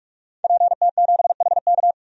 Morse code, heard as a keyed tone, about 48 wpm, 700 Hz. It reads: PT7HK
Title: DXCC-100 Call Signs - 5 Characters - 45wpm